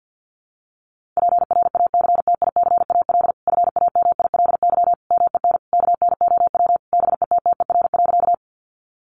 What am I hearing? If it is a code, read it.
Morse code, heard as a keyed tone, about 49 wpm, 700 Hz. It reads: PRACTICAL FAMILY MEN KNOW BETTER.
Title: Sentences from Top 1400 Words 50wpm